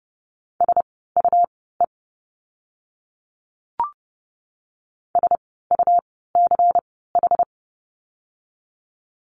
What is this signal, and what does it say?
Morse code, heard as a keyed tone, about 30 wpm, spaced as for 17 wpm, 700 Hz. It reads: SUE SUCH
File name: Lesson 17f - Letter u - Words 30wpm